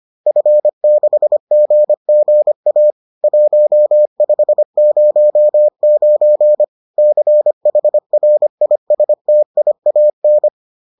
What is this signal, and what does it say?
Morse code, heard as a keyed tone, about 25 wpm, 600 Hz. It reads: F6GGA 1509 CHRISTIAN
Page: bulletin UFT sous forme texte et en CW